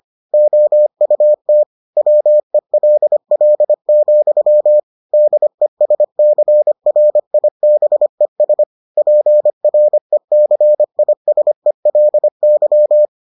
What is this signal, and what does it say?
Morse code, heard as a keyed tone, about 25 wpm, 600 Hz. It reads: OUT WELL, DESCRIBES PRECISELY